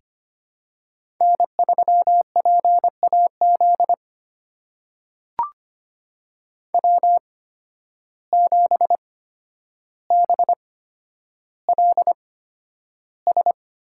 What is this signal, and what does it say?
Morse code, heard as a keyed tone, about 25 wpm, spaced as for 11 wpm, 700 Hz. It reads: N3PAZ W7BLS
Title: USA Call Signs - 5 Characters - 8x Character Spacing - 25wpm